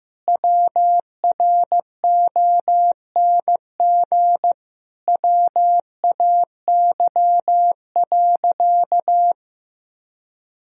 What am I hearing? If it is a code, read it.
Morse code, heard as a keyed tone, about 15 wpm, 700 Hz. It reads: WRONG WAY.
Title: Sentences from Top 1000 Words - Review Part 1 - 15wpm